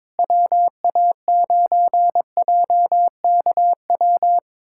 Morse code, heard as a keyed tone, about 22 wpm, 700 Hz. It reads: WA9JKW
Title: USA Callsign DE Callsign - 6 Characters - 22wpm